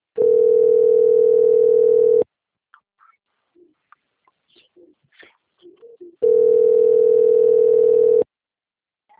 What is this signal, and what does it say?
Ringback tone (North America), 440 + 480 Hz.
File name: rg-1600-unknown- - - .29426